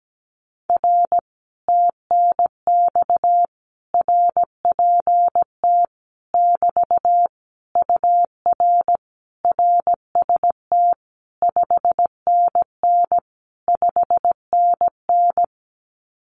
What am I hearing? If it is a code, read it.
Morse code, heard as a keyed tone, about 17 wpm, 700 Hz. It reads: R TNX RPT <BT> UR RST 5NN 5NN
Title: Common QSO Phrases Encore 17wpm